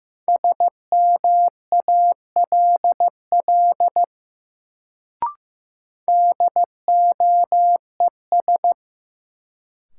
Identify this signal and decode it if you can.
Morse code, heard as a keyed tone, about 15 wpm, 700 Hz. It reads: SMALL DOES
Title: Sets of 2 Words - New Word Drill - 15wpm